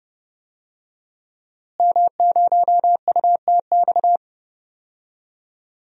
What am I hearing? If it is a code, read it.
Morse code, heard as a keyed tone, about 31 wpm, 700 Hz. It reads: M0UTX
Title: DXCC-100 Call Signs - 5 Characters - Repeated 2x - 30wpm